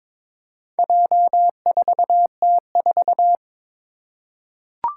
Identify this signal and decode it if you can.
Morse code, heard as a keyed tone, about 22 wpm, 700 Hz. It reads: J4T4
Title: Four Challenging Letter-Number Combinations 22wpm